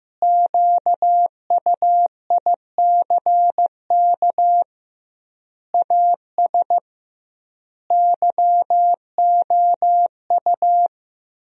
Morse code, heard as a keyed tone, about 15 wpm, 700 Hz. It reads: QUICK AS YOU
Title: 2x Word-Spacing - Sentences from Top 500 Words - 15wpm